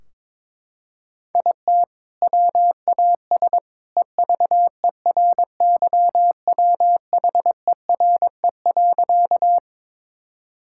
Morse code, heard as a keyed tone, about 22 wpm, 700 Hz. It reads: IT WAS EVERYWHERE.
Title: Sentences from Top 1600 Words 22wpm